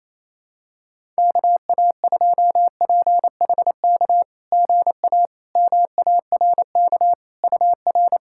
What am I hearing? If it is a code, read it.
Morse code, heard as a keyed tone, about 27 wpm, 690 Hz. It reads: KA2PHK GA MARK UR
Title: SOTA QSO Practice 28wpm